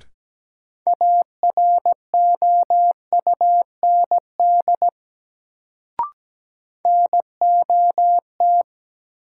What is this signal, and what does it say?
Morse code, heard as a keyed tone, about 17 wpm, 700 Hz. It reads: AROUND NOT
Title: Sentences from Top 600 Words - New Word Drill - 17wpm